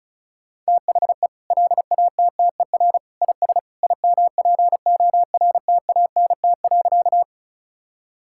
Morse code, heard as a keyed tone, about 34 wpm, 700 Hz. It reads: THE LATTER IS IMPORTANT.
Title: Sentences from Top 1600 Words 35wpm